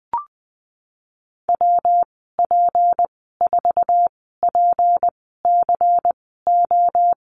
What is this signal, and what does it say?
Morse code, heard as a keyed tone, about 19 wpm, spaced as for 14 wpm, 700 Hz. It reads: WP4PCO
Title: 238 Lesson 38g - Letter q - Call Signs 20wpm